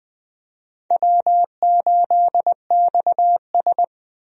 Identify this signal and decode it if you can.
Morse code, heard as a keyed tone, about 20 wpm, 700 Hz. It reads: W8XS
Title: USA Call Signs - 4 Characters - Repeated 3x - 20wpm